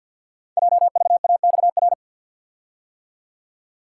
Morse code, heard as a keyed tone, about 53 wpm, 700 Hz. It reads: JVAXR
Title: Code Groups - 5 Letter:Numbers - 50wpm